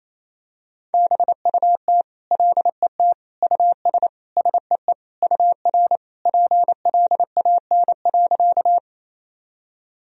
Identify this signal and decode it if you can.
Morse code, heard as a keyed tone, about 28 wpm, 700 Hz. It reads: BUT LET US SEE UR PLAN.
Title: Sentences from Top 500 Words and Abbreviations 28wpm